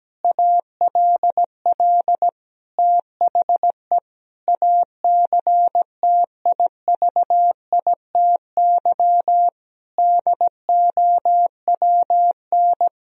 Morse code, heard as a keyed tone, about 17 wpm, 700 Hz. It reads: ALL THE ACTIVITY DOWN
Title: Sentences from Top 1000 Words - Review Part 4 - 17wpm